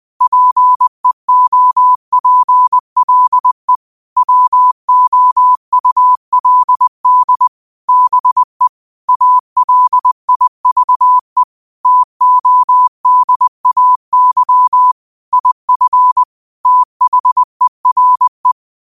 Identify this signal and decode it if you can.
Morse code, heard as a keyed tone, about 20 wpm, 1000 Hz. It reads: PEOPLE WOULD BE ALIVE TODAY IF THERE
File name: 20wpm